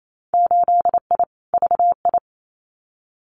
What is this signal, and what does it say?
Morse code, heard as a keyed tone, about 28 wpm, 700 Hz. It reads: 8I VI
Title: Winter Field Day Exchanges Encore 28wpm